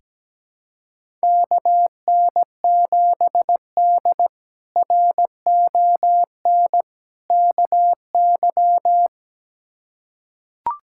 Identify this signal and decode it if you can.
Morse code, heard as a keyed tone, about 17 wpm, 700 Hz. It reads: KN7D RON KY
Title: K1USN SST QSO Practice - 17wpm